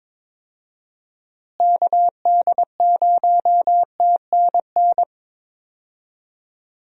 Morse code, heard as a keyed tone, about 22 wpm, 700 Hz. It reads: KD0TNN
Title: USA Call Signs - 6 Characters - Repeated 3x - 22wpm